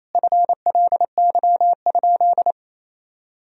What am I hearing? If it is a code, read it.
Morse code, heard as a keyed tone, about 28 wpm, 700 Hz. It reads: FLY?
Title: Sentences from Top 1000 Words - Review Part 2 - 28wpm